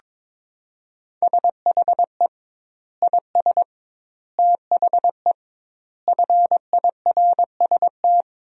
Morse code, heard as a keyed tone, about 23 wpm, 700 Hz. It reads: SHE IS THE FIRST
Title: 2x Word-Spacing - Sentences from Top 400 Words - 22wpm